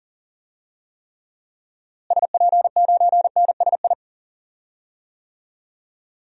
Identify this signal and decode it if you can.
Morse code, heard as a keyed tone, about 40 wpm, 700 Hz. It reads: SP9NSI
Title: DXCC-100 Call Signs - 6 Characters - Repeated 2x - 40wpm